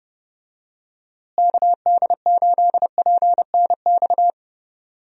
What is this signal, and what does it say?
Morse code, heard as a keyed tone, about 30 wpm, 700 Hz. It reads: KD8PNX
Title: USA Call Signs - 6 Characters - Repeated 3x - 30wpm